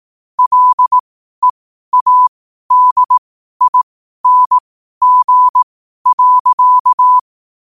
Morse code, heard as a keyed tone, about 18 wpm, spaced as for 12 wpm, 1000 Hz. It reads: LEADING.
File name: qotdfiles / 12wpm